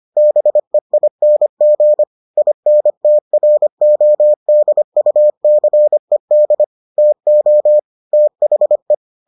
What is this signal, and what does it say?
Morse code, heard as a keyed tone, about 25 wpm, 600 Hz. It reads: BEING INTRODUCED TO THE